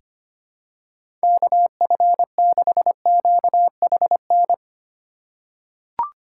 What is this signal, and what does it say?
Morse code, heard as a keyed tone, about 25 wpm, 700 Hz. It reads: KF6QHN
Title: USA Call Signs - 6 Characters - 2x Character Spacing - 25wpm